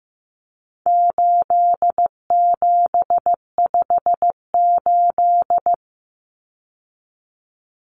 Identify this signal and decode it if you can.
Morse code, heard as a keyed tone, about 15 wpm, 700 Hz. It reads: 8758
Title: Numbers - 4-Digit 15wpm